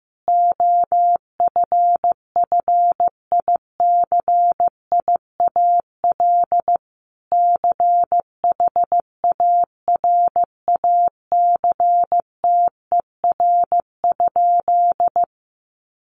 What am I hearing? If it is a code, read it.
Morse code, heard as a keyed tone, about 15 wpm, 700 Hz. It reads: OFFICIAL CHARACTER?
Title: Sentences from Top 1200 Words 15wpm